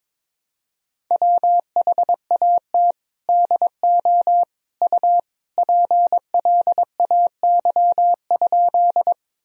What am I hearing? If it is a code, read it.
Morse code, heard as a keyed tone, about 21 wpm, 700 Hz. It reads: WHAT DO U PLAY?
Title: Sentences from Top 200 Words and Abbreviations 22wpm